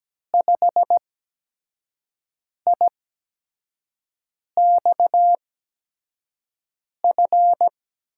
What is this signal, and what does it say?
Morse code, heard as a keyed tone, about 17 wpm, spaced as for 5 wpm, 700 Hz. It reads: 5IXF
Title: USA Call Signs - 6 Characters - 8x Character Spacing - 17wpm